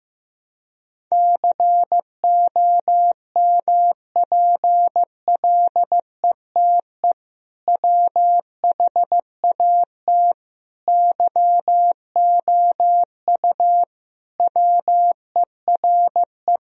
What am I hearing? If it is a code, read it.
Morse code, heard as a keyed tone, about 15 wpm, 700 Hz. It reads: COMPLETE WHAT YOU WERE
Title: Sentences from Top 1000 Words - Review Part 2 - 15wpm